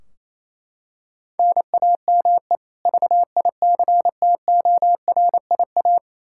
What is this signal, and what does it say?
Morse code, heard as a keyed tone, about 28 wpm, 700 Hz. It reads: NAME VICTORIA
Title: Practice QSO - Exchange 1 - Beginning - 28wpm